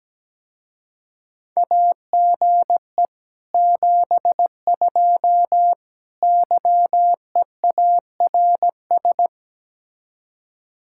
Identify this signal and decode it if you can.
Morse code, heard as a keyed tone, about 19 wpm, 700 Hz. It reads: AGE 72 YEARS
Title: Speed-Racing Common QSO Phrases - 17wpm